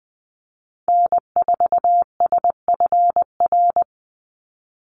Morse code, heard as a keyed tone, about 20 wpm, 700 Hz. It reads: N4SFR
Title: NA SOTA Activator Callsigns 20wpm